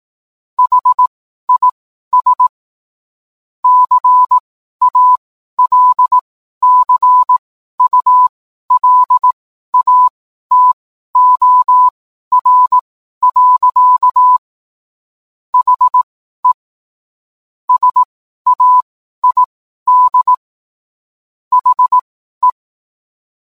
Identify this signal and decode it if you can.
Morse code, heard as a keyed tone, about 18 wpm, spaced as for 12 wpm, 1000 Hz. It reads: HIS CALCULATOR. HE SAID HE